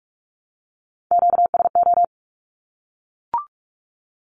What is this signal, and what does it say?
Morse code, heard as a keyed tone, about 44 wpm, 700 Hz. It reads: QSO
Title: Q-Codes 45wpm